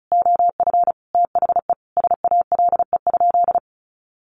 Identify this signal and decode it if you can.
Morse code, heard as a keyed tone, about 34 wpm, 700 Hz. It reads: OF THE SALE?
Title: Sentences from Top 1400 Words 35wpm